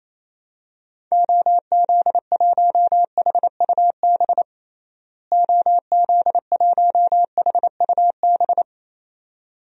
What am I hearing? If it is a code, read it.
Morse code, heard as a keyed tone, about 28 wpm, 700 Hz. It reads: OZ1HUB OZ1HUB
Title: DXCC-100 Call Signs - 6 Characters - Repeated 2x - 28wpm